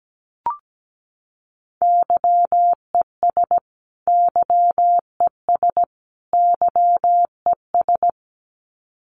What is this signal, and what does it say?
Morse code, heard as a keyed tone, about 17 wpm, 700 Hz. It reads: YES YES YES
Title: Top 500 Words Repeated 17wpm